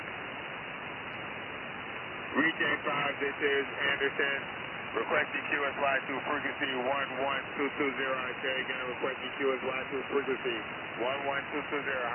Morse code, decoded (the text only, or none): none